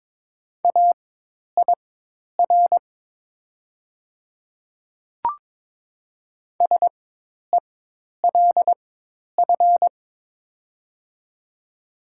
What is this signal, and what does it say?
Morse code, heard as a keyed tone, about 22 wpm, spaced as for 10 wpm, 700 Hz. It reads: AIR SELF
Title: 4x Character Spacing - Top 300 Words - 22wpm